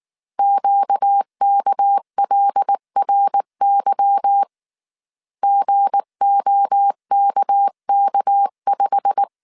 Morse code, heard as a keyed tone, about 19 wpm, 800 Hz. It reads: QKLRY GOKK5
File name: CW / DET / 1 9 1500C S B 18wpm 2551